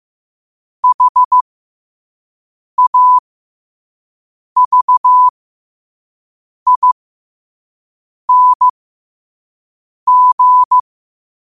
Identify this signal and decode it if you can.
Morse code, heard as a keyed tone, about 15 wpm, spaced as for 5 wpm, 1000 Hz. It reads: HAVING
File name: qotdfiles / 5wpm